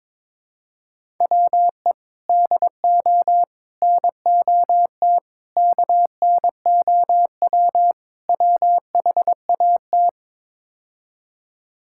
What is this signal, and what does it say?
Morse code, heard as a keyed tone, about 22 wpm, 700 Hz. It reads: WE DO NOT KNOW WHAT
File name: practice / 22wpm / Build A Phrase - 5-Word Phrases - 22wpm